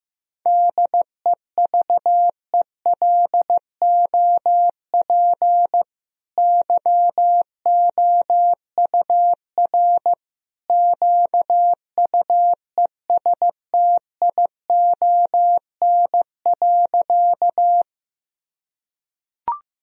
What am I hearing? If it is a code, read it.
Morse code, heard as a keyed tone, about 15 wpm, 700 Hz. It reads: DEVELOP YOUR QUESTION.
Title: Sentences from Top 1000 Words - Review Part 2 - 15wpm